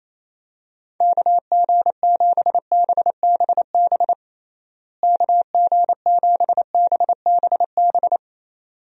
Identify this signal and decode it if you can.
Morse code, heard as a keyed tone, about 28 wpm, 700 Hz. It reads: KG7BBB KG7BBB
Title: USA Call Signs - 6 Characters - Repeated 2x - 28wpm